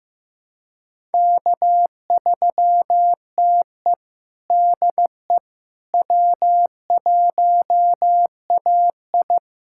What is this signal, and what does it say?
Morse code, heard as a keyed tone, about 15 wpm, 700 Hz. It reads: K3TE DE W1AI
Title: USA Callsign DE Callsign - 4 Characters - 15wpm